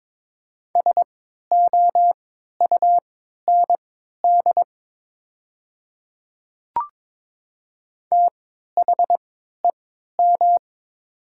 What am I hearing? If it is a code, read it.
Morse code, heard as a keyed tone, about 22 wpm, spaced as for 12 wpm, 700 Hz. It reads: SOUND THEM
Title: 3x Character Spacing - Top 100 Words - 22wpm